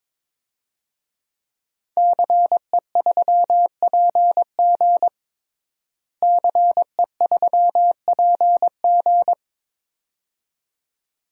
Morse code, heard as a keyed tone, about 22 wpm, 700 Hz. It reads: CE3PG CE3PG
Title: DXCC-100 Call Signs - 5 Characters - Repeated 2x - 22wpm